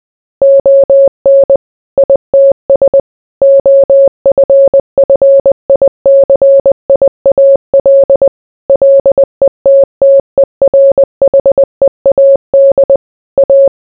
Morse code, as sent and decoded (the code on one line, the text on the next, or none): --- -. / .. - ... / --- ..-. ..-. .. -.-. .. .- .-.. / .-.. - - . .-. .... .- -.. / .-
ON ITS OFFICIAL LETTERHEAD A